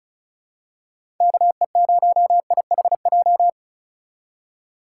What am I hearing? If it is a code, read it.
Morse code, heard as a keyed tone, about 36 wpm, 700 Hz. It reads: KE0IHJ
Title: USA Call Signs - 6 Characters - 3x Character Spacing - 35wpm